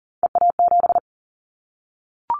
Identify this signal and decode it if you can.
Morse code, heard as a keyed tone, about 40 wpm, 700 Hz. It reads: EA7
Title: Callsign Prefixes for Top 25 DXCC - Repeated 3x - 40wpm